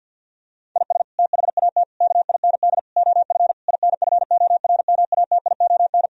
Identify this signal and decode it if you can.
Morse code, heard as a keyed tone, about 50 wpm, 700 Hz. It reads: IS THAT KIND OF INFORMATION
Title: Sentences from Top 1000 Words - Review Part 3 - 50wpm